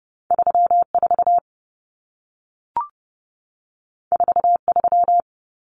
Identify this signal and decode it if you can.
Morse code, heard as a keyed tone, about 30 wpm, 700 Hz. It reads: 34 43
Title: Common Cut Numbers - 2-Digit 30wpm